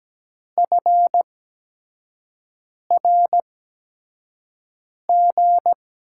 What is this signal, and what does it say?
Morse code, heard as a keyed tone, about 17 wpm, spaced as for 5 wpm, 700 Hz. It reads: FRG
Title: USA Call Signs - 6 Characters - 8x Character Spacing - 17wpm